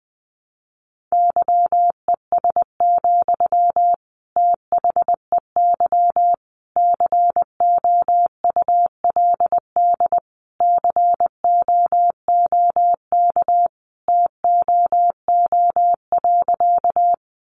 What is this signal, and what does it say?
Morse code, heard as a keyed tone, about 20 wpm, 700 Hz. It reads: YES, THEY COULD COOK TOO.
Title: Sentences from Top 700 Words 20wpm